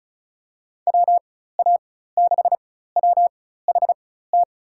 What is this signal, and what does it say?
Morse code, heard as a keyed tone, about 36 wpm, spaced as for 16 wpm, 700 Hz. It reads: WA6WHT
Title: USA Call Signs - 6 Characters - 4x Character Spacing - 35wpm